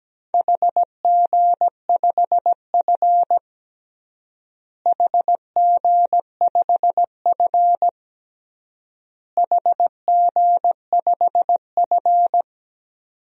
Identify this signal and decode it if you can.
Morse code, heard as a keyed tone, about 17 wpm, 700 Hz. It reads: HG5F HG5F HG5F
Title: DXCC-100 Call Signs - 4 Characters - Repeated 3x - 17wpm